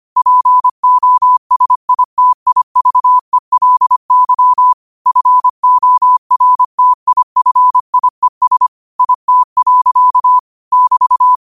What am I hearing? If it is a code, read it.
Morse code, heard as a keyed tone, about 25 wpm, 1000 Hz. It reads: POSITIVELY FORTIFIES IT. <BT>
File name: qotdfiles / 25wpm